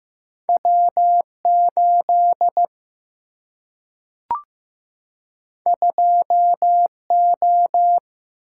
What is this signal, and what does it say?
Morse code, heard as a keyed tone, about 15 wpm, 700 Hz. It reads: W 8 2 O
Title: Single-Letter Single-Number 15wpm